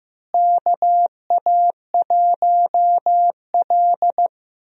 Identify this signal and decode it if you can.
Morse code, heard as a keyed tone, about 15 wpm, 700 Hz. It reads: KA1L
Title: USA Call Signs - 4 Characters - Repeated 2x - 15wpm